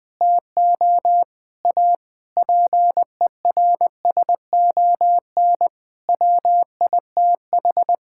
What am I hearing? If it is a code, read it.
Morse code, heard as a keyed tone, about 20 wpm, 700 Hz. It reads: TO A PERSON WITH
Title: Sets of 9 Words - 20wpm